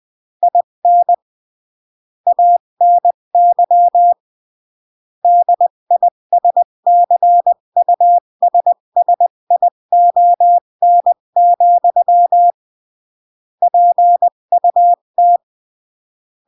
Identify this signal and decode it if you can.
Morse code, heard as a keyed tone, about 20 wpm, 700 Hz. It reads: IN ANY DISCUSSION, PUT
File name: CW / jodorowsky / 82 commandments 20-15